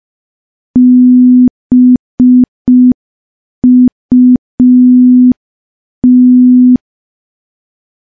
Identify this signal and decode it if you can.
Morse code, heard as a keyed tone, about 5 wpm, 250 Hz. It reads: BUT
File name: morse / morse-training-05WPM